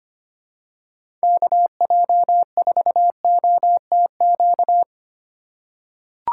Morse code, heard as a keyed tone, about 25 wpm, 700 Hz. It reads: KJ4OTQ
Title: USA Call Signs - 6 Characters - 8x Character Spacing - 25wpm